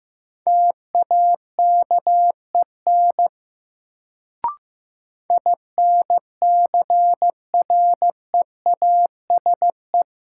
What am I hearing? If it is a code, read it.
Morse code, heard as a keyed tone, about 15 wpm, 700 Hz. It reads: TAKEN INCREASE
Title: Sentences from Top 1000 Words - Review Part 4 - New Word Drill - 15wpm